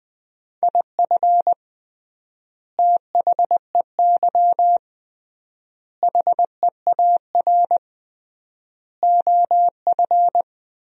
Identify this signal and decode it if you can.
Morse code, heard as a keyed tone, about 21 wpm, 700 Hz. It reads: IF THEY HEAR OF
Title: 3x Word-Spacing - Sentences from Top 400 Words - 20wpm